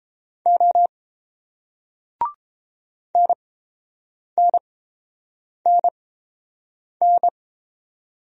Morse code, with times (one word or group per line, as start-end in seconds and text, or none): 0.5-0.9 O
3.2-7.3 NNNN